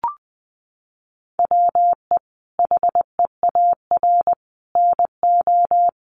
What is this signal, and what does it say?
Morse code, heard as a keyed tone, about 20 wpm, 700 Hz. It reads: WE HEAR NO
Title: Sentences from Top 1400 Words 20wpm